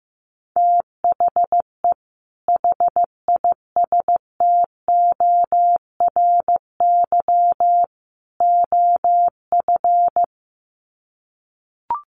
Morse code, heard as a keyed tone, about 15 wpm, 700 Hz. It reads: THE HISTORY OF
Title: Sets of 3 Words 15wpm